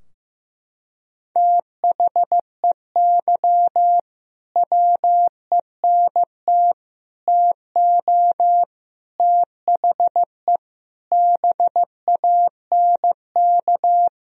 Morse code, heard as a keyed tone, about 15 wpm, 700 Hz. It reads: THEY WENT TO THE BANK
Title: Sentences from Top 1000 Words - Review Part 4 - 15wpm